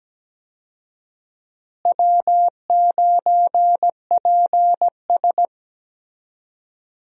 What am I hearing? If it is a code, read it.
Morse code, heard as a keyed tone, about 17 wpm, 700 Hz. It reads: W9PS
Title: USA Call Signs - 4 Characters - Repeated 3x - 17wpm